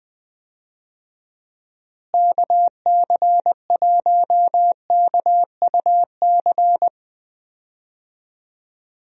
Morse code, heard as a keyed tone, about 21 wpm, 700 Hz. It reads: KC1KUC
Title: USA Call Signs - 6 Characters - Repeated 3x - 20wpm